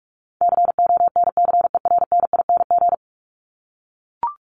Morse code, heard as a keyed tone, about 44 wpm, 700 Hz. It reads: CONCERNING
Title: Top 1751-2000 Words 45wpm